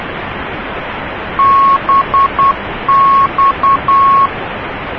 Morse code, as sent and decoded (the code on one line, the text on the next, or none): -... -..-
BX